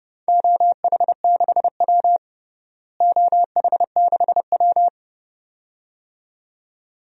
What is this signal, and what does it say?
Morse code, heard as a keyed tone, about 32 wpm, 700 Hz. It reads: OH6W OH6W
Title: DXCC-100 Call Signs - 4 Characters - Repeated 2x - 30wpm